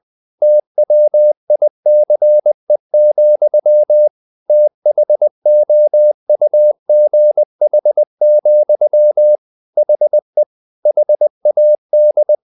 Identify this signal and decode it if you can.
Morse code, heard as a keyed tone, about 20 wpm, 600 Hz. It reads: TWICE, THOUGH, HE HAD
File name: the old man and the sea part 2 20wpm